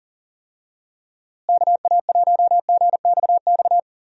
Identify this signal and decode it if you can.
Morse code, heard as a keyed tone, about 40 wpm, 700 Hz. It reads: KA1GXX
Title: USA Call Signs - 6 Characters - Repeated 2x - 40wpm